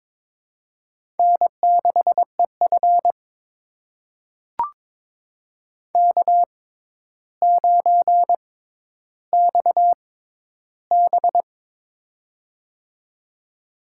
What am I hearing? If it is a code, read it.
Morse code, heard as a keyed tone, about 22 wpm, spaced as for 11 wpm, 700 Hz. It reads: N6EF K9XB
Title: USA Call Signs - 4 Characters - 6x Character Spacing - 22wpm